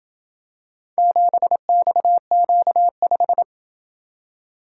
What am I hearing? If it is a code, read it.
Morse code, heard as a keyed tone, about 27 wpm, 700 Hz. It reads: 7XQ5